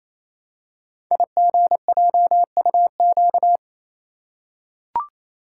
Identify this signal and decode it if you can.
Morse code, heard as a keyed tone, about 27 wpm, 700 Hz. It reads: IGJUQ
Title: Code Groups - 5 Letters - 28wpm